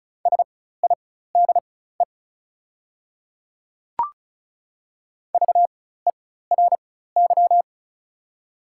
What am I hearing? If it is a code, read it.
Morse code, heard as a keyed tone, about 36 wpm, spaced as for 16 wpm, 700 Hz. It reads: SIDE VERY
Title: 4x Character Spacing - Top 200 Words - 35wpm